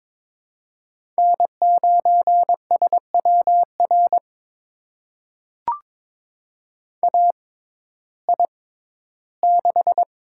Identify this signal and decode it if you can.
Morse code, heard as a keyed tone, about 22 wpm, spaced as for 13 wpm, 700 Hz. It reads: N9SWR AI6
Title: USA Call Signs - 5 Characters - 6x Character Spacing - 22wpm